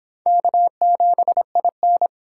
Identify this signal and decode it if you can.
Morse code, heard as a keyed tone, about 25 wpm, 700 Hz. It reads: K7IN